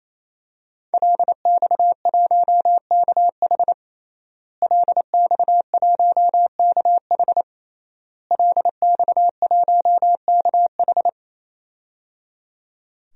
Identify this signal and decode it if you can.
Morse code, heard as a keyed tone, about 27 wpm, 700 Hz. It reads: LX1KH LX1KH LX1KH
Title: DXCC-100 Call Signs - 5 Characters - Repeated 3x - 28wpm